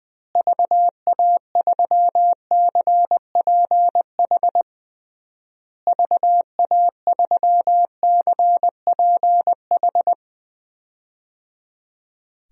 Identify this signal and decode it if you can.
Morse code, heard as a keyed tone, about 20 wpm, 700 Hz. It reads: VA3CPH VA3CPH
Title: DXCC-100 Call Signs - 6 Characters - Repeated 3x - 20wpm